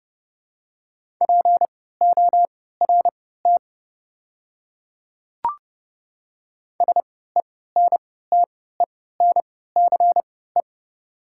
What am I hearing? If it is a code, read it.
Morse code, heard as a keyed tone, about 30 wpm, spaced as for 17 wpm, 700 Hz. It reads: PORT SENTENCE
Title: 3x Character Spacing - Top 300 Words - 30wpm